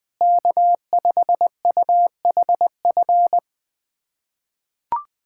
Morse code, heard as a keyed tone, about 21 wpm, 700 Hz. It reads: K5UHF
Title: USA Callsign DE Callsign - 5 Characters - 20wpm